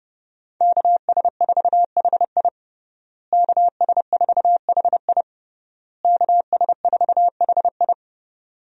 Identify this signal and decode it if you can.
Morse code, heard as a keyed tone, about 30 wpm, 700 Hz. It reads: KS4HI KS4HI KS4HI
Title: USA Call Signs - 5 Characters - Repeated 3x - 30wpm